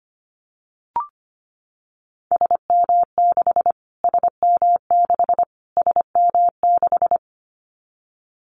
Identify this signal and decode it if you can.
Morse code, heard as a keyed tone, about 25 wpm, 700 Hz. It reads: SM6 SM6 SM6
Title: Callsign Prefixes for Top 25 DXCC - Repeated 3x - 25wpm